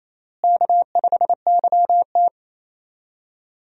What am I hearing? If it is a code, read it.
Morse code, heard as a keyed tone, about 28 wpm, 700 Hz. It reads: K5YT
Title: USA Call Signs - 4 Characters - Repeated 2x - 28wpm